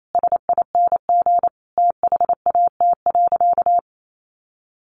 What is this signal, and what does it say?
Morse code, heard as a keyed tone, about 28 wpm, 700 Hz. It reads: SING THAT.
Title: Sentences from Top 500 Words 28wpm